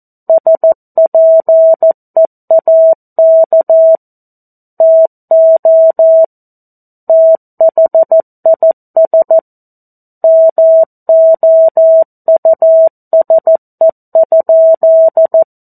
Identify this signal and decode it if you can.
Morse code, heard as a keyed tone, about 14 wpm, 650 Hz. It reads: SPEAK TO THIS MOUSE?
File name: 02-Chapter-02 14 WPM